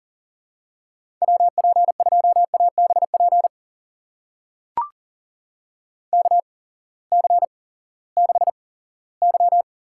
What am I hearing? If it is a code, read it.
Morse code, heard as a keyed tone, about 40 wpm, spaced as for 20 wpm, 700 Hz. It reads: WP2ABP KC6Y
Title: USA Call Signs - 6 Characters - 8x Character Spacing - 40wpm